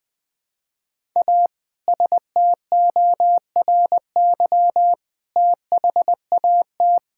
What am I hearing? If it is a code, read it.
Morse code, heard as a keyed tone, about 21 wpm, 700 Hz. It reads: A STORY THAT
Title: Sentences with 2-Word Phrases - 20wpm